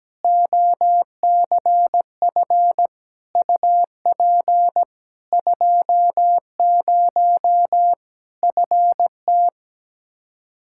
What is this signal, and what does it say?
Morse code, heard as a keyed tone, about 17 wpm, 700 Hz. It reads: OCF UP 20 FT
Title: Practice QSO - Exchange 2 - Info - 17wpm